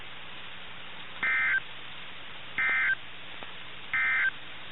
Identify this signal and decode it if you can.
Emergency Alert System (SAME) tones, heard as harsh buzzing bursts. It is heard from 1.1 to 4.4 s.
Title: NOAA Weather Radio - KIH61